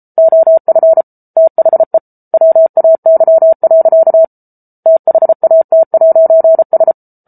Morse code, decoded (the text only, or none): OF THE WAY. THAT'S